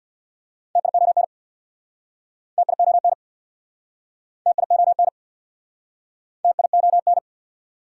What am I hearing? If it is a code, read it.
Morse code, heard as a keyed tone, about 69 wpm, 700 Hz. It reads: TION TION TION TION